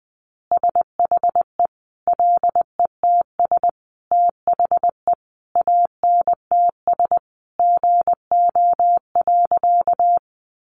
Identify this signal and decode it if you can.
Morse code, heard as a keyed tone, about 20 wpm, 700 Hz. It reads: SHE LETS THE ANTS GO.
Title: Easy Phrases 20wpm